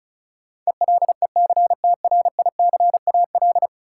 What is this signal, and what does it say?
Morse code, heard as a keyed tone, about 36 wpm, 700 Hz. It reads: ELECTRICAL
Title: Sentences from Top 1000 Words - Review Part 6 - New Word Drill - 35wpm